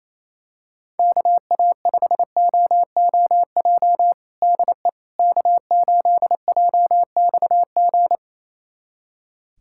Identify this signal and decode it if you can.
Morse code, heard as a keyed tone, about 28 wpm, 700 Hz. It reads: KA5OOJ DE K8JXG
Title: USA Callsign DE Callsign 28wpm